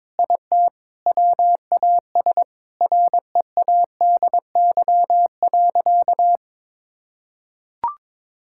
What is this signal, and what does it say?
Morse code, heard as a keyed tone, about 21 wpm, 700 Hz. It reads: IT WAS READY.
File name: practice / 22wpm / Sentences with 3-Word Phrases - 22wpm